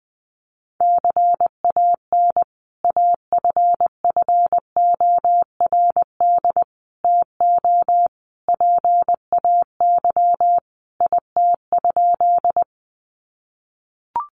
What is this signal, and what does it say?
Morse code, heard as a keyed tone, about 20 wpm, 700 Hz. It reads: CAN AFFORD TO PAY IT?
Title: Sentences from Top 1100 Words 20wpm